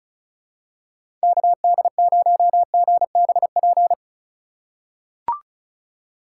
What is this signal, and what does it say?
Morse code, heard as a keyed tone, about 36 wpm, 700 Hz. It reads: KD0GBP
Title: USA Call Signs - 6 Characters - 4x Character Spacing - 35wpm